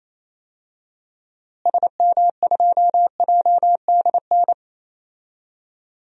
Morse code, heard as a keyed tone, about 28 wpm, 700 Hz. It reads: SM2JDN
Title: DXCC-100 Call Signs - 6 Characters - Repeated 2x - 28wpm